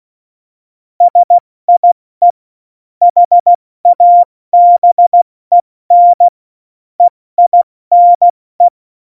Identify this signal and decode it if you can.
Morse code, heard as a keyed tone, about 16 wpm, 700 Hz. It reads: SIE HABEN EINE
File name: Morsekurs / Witze / Witz 02. Januar 2018, 70 F80